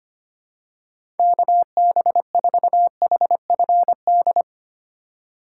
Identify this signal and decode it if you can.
Morse code, heard as a keyed tone, about 25 wpm, 700 Hz. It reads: KB4HFD
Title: USA Call Signs - 6 Characters - 2x Character Spacing - 25wpm